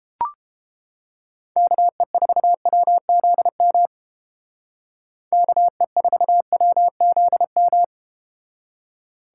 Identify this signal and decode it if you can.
Morse code, heard as a keyed tone, about 31 wpm, 700 Hz. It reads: KE4WZM KE4WZM